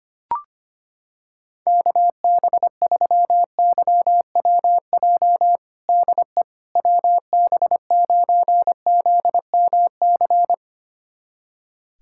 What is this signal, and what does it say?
Morse code, heard as a keyed tone, about 25 wpm, 700 Hz. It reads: KB3YWJ DE WB9ZMC
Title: USA Callsign DE Callsign 25wpm